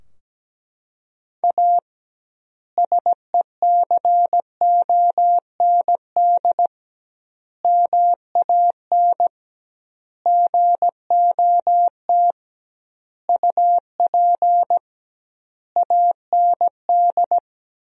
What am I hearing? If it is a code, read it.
Morse code, heard as a keyed tone, about 17 wpm, 700 Hz. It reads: A SECOND MAN GOT UP AND
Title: 2x Word-Spacing - Sentences from Top 300 Words - 17wpm